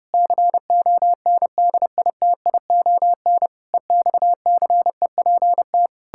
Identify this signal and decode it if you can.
Morse code, heard as a keyed tone, about 32 wpm, 700 Hz. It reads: CONDITION EXCEPT
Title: Pangram Challenge v2 - 30wpm